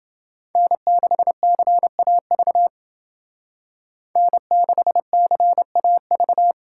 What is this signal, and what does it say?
Morse code, heard as a keyed tone, about 28 wpm, 700 Hz. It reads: N6CAV N6CAV